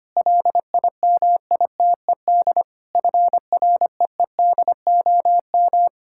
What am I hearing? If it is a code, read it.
Morse code, heard as a keyed tone, about 25 wpm, 700 Hz. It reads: LIMITED FREEDOM
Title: Sentences from Top 1000 Words - Review Part 6 - 25wpm